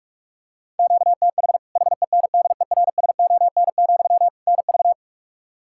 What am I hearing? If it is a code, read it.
Morse code, heard as a keyed tone, about 46 wpm, 700 Hz. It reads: QTH HENDERSON, NV
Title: Practice QSO - Exchange 1 - Beginning - 45wpm